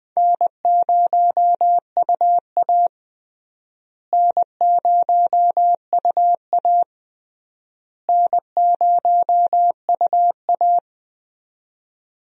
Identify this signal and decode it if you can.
Morse code, heard as a keyed tone, about 21 wpm, 700 Hz. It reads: N0UA N0UA N0UA
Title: USA Call Signs - 4 Characters - Repeated 3x - 20wpm